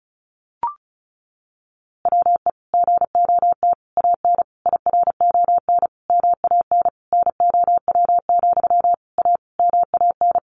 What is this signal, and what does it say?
Morse code, heard as a keyed tone, about 34 wpm, 700 Hz. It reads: WE GOT AN IRON MAN NOW, A MAN
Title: A Sound Of Thunder 35wpm